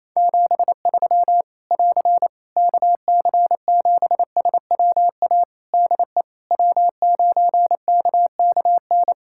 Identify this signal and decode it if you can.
Morse code, heard as a keyed tone, about 28 wpm, 700 Hz. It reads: 73 <AR> KC7SWA DE W9KKN
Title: Practice QSO - Exchange 4 - Ending - 28wpm